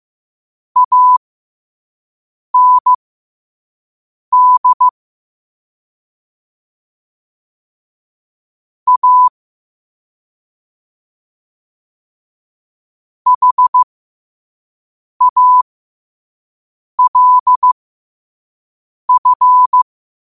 Morse code, heard as a keyed tone, about 15 wpm, spaced as for 5 wpm, 1000 Hz. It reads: AND A HALF
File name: qotdfiles / 5wpm